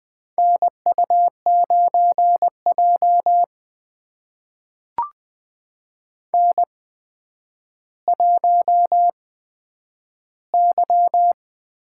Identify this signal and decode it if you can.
Morse code, heard as a keyed tone, about 20 wpm, spaced as for 10 wpm, 700 Hz. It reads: NU9J N1Y
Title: USA Call Signs - 4 Characters - 8x Character Spacing - 20wpm